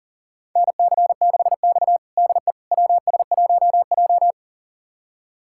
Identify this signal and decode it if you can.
Morse code, heard as a keyed tone, about 40 wpm, 700 Hz. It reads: NC6X DE WS1J
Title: USA Callsign DE Callsign - 4 Characters - 40wpm